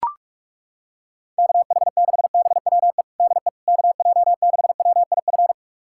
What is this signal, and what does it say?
Morse code, heard as a keyed tone, about 41 wpm, 700 Hz. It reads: KH6BWE DE KJ6WIF
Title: USA Callsign DE Callsign 45wpm